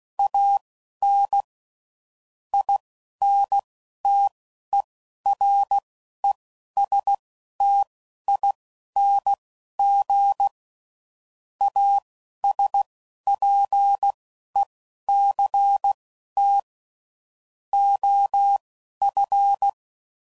Morse code, heard as a keyed tone, about 16 wpm, spaced as for 11 wpm, 800 Hz. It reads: AN INTERESTING ASPECT OF
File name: CW Code Groups @10WPM / Technical Subjects / SwPwrSup